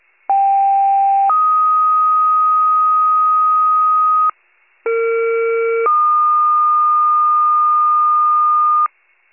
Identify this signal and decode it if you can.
Two-tone page, heard as a one-second tone, then a three-second tone, about 790 then 1250 Hz, about 460 then 1220 Hz. It is heard from 0.3 to 8.9 s.